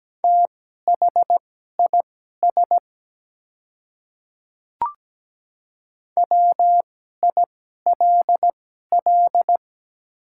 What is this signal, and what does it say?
Morse code, heard as a keyed tone, about 17 wpm, spaced as for 12 wpm, 700 Hz. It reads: THIS WILL
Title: 2x Character Spacing - Top 200 Words - 17wpm